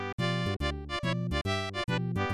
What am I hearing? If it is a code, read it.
There is no code.